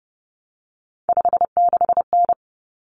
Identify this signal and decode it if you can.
Morse code, heard as a keyed tone, about 30 wpm, 700 Hz. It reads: 56N
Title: Common Cut Numbers - 3-Digit 30wpm